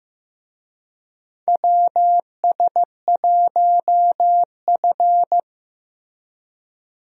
Morse code, heard as a keyed tone, about 15 wpm, 700 Hz. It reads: WS1F
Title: USA Call Signs - 4 Characters - Repeated 2x - 15wpm